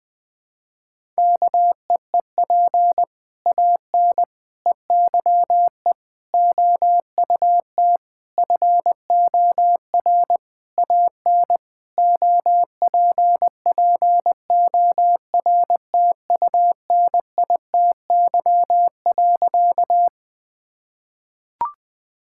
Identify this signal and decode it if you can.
Morse code, heard as a keyed tone, about 20 wpm, 700 Hz. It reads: KEEP AN EYE OUT FOR AN OPPORTUNITY.
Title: Sentences with 6-Word Phrases - Spoken Beginning - 20wpm